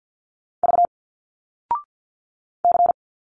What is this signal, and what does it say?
Morse code, heard as a keyed tone, about 50 wpm, 700 Hz. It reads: V C
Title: Single Letters 50wpm